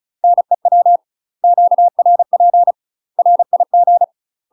Morse code, heard as a keyed tone, about 34 wpm, 700 Hz. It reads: NEW QRP RIG
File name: UK MASTER QSOs TRAINING / 35WPM / UK MASTER QSOs 0022